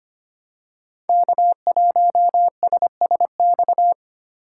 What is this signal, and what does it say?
Morse code, heard as a keyed tone, about 25 wpm, 700 Hz. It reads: K1SSX
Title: Code Groups - 5 Letter:Numbers - 25wpm